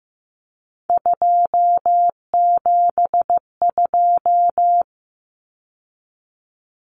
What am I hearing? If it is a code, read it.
Morse code, heard as a keyed tone, about 15 wpm, 700 Hz. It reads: 272
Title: Numbers - 3-Digit 15wpm